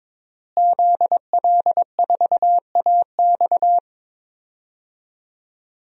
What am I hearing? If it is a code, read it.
Morse code, heard as a keyed tone, about 22 wpm, 700 Hz. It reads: ZL4AX
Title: DXCC-100 Call Signs - 5 Characters - Repeated 3x - 22wpm